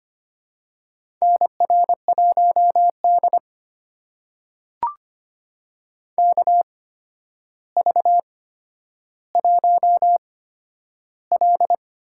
Morse code, heard as a keyed tone, about 25 wpm, spaced as for 11 wpm, 700 Hz. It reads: NR1D KV1L
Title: USA Call Signs - 4 Characters - 8x Character Spacing - 25wpm